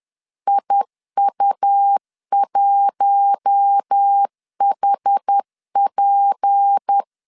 Morse code, heard as a keyed tone, about 10 wpm, 800 Hz. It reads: IU1HP